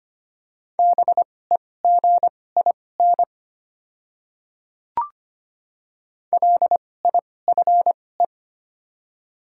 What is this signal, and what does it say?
Morse code, heard as a keyed tone, about 25 wpm, spaced as for 18 wpm, 690 Hz. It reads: BEGIN LIFE
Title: 2x Character Spacing - Top 300 Words - 25wpm